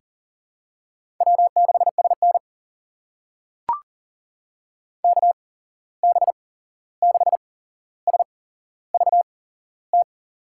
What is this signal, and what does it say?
Morse code, heard as a keyed tone, about 40 wpm, spaced as for 15 wpm, 700 Hz. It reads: W6SN KB6SVT
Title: USA Call Signs - 8x Character Spacing - 40wpm